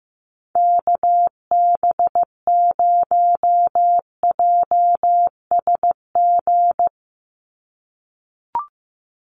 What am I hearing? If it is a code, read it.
Morse code, heard as a keyed tone, about 15 wpm, 700 Hz. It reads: KB0JSG
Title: USA Call Signs 15wpm